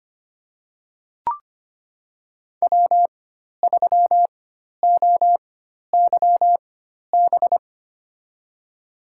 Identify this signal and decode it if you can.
Morse code, heard as a keyed tone, about 25 wpm, spaced as for 12 wpm, 700 Hz. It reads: W3OYB
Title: USA Call Signs - 5 Characters - 4x Character Spacing - 25wpm